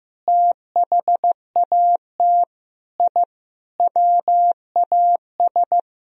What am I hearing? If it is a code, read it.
Morse code, heard as a keyed tone, about 15 wpm, 700 Hz. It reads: THAT I WAS
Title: Sentences from Top 1000 Words - Review Part 1 - 15wpm